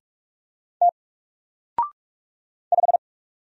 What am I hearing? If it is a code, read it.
Morse code, heard as a keyed tone, about 44 wpm, 700 Hz. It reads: T 5
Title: Single Letter-Number 45wpm